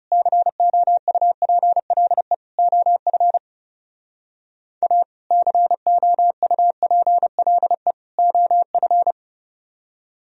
Morse code, heard as a keyed tone, about 36 wpm, 700 Hz. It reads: COUPLE OF A COUPLE OF
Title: Speed-Racing - Sets of 3 Words 30wpm